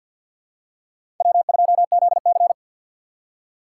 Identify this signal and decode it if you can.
Morse code, heard as a keyed tone, about 51 wpm, 700 Hz. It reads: W2ZC
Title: USA Call Signs - 3x Character Spacing - 50wpm